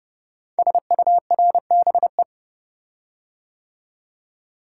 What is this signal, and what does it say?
Morse code, heard as a keyed tone, about 30 wpm, 700 Hz. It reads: SURBE
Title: Code Groups - 5 Letters - 30wpm